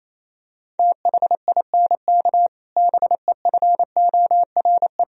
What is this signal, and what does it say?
Morse code, heard as a keyed tone, about 28 wpm, 700 Hz. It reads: THINK BEFORE
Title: Sentences from Top 1000 Words - Review Part 1 - 28wpm